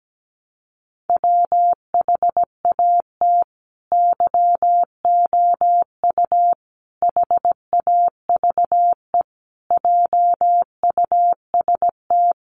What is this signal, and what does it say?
Morse code, heard as a keyed tone, about 17 wpm, 700 Hz. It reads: WHAT YOU HAVE JUST
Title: Sentences from Top 200 Words 17wpm